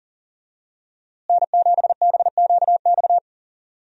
Morse code, heard as a keyed tone, about 40 wpm, 700 Hz. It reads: N7BQX
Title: USA Call Signs - 5 Characters - Repeated 3x - 40wpm